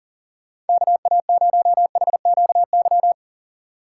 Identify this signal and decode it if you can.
Morse code, heard as a keyed tone, about 51 wpm, 700 Hz. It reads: KA0HQY